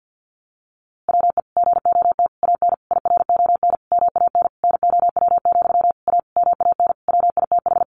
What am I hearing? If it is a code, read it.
Morse code, heard as a keyed tone, about 49 wpm, 700 Hz. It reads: WE GOT AN IRON MAN NOW, A MAN WITH
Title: A Sound Of Thunder 50wpm